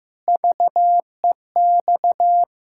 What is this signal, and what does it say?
Morse code, heard as a keyed tone, about 16 wpm, 700 Hz. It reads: VEX
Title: Pangram Challenge v2 - 15wpm